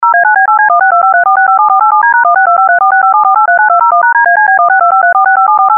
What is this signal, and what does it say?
DTMF, heard as telephone touch tones, tones of about 120 ms. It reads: 0A9B8C162534657487D#1625346574839201#DACB16253465748